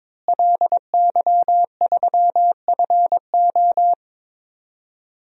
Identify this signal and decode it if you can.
Morse code, heard as a keyed tone, about 22 wpm, 700 Hz. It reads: LY3FO
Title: DXCC-100 Call Signs - 5 Characters - Repeated 2x - 22wpm